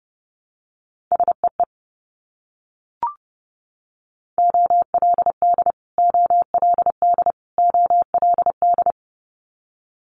Morse code, heard as a keyed tone, about 30 wpm, 700 Hz. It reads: SEE OLD OLD OLD
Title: Multi-letter Phonemes Repeated 3x 30wpm